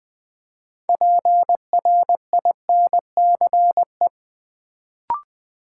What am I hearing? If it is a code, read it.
Morse code, heard as a keyed tone, about 20 wpm, 700 Hz. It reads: PRINCE
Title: Build A Word - 3 Letters At A Time - 6-Letter Words - 20wpm